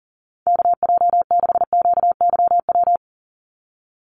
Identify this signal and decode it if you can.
Morse code, heard as a keyed tone, about 38 wpm, 700 Hz. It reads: KJ6QYW
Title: USA Call Signs 40wpm